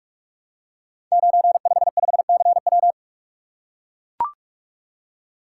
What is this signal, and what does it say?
Morse code, heard as a keyed tone, about 44 wpm, 690 Hz. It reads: 955CW
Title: Code Groups - 5 Letter:Numbers - 45wpm